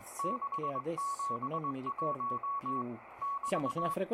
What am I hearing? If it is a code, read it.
Morse code, heard as a keyed tone, about 20 wpm, 1120 Hz, with speech and radio noise. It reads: 671 <BT>